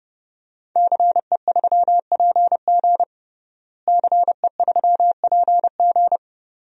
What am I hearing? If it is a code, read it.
Morse code, heard as a keyed tone, about 30 wpm, 700 Hz. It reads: CE3PG CE3PG
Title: DXCC-100 Call Signs - 5 Characters - Repeated 2x - 30wpm